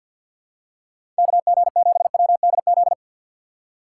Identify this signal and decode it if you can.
Morse code, heard as a keyed tone, about 53 wpm, 700 Hz. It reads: K G 7 W D Z
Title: USA Call Signs - 6 Characters - 8x Character Spacing - 50wpm